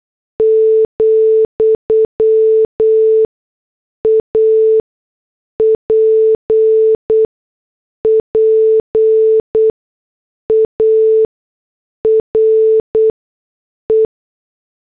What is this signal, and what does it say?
Morse code, heard as a keyed tone, about 8 wpm, spaced as for 6 wpm, 440 Hz. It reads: , APPARE
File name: morse / headlines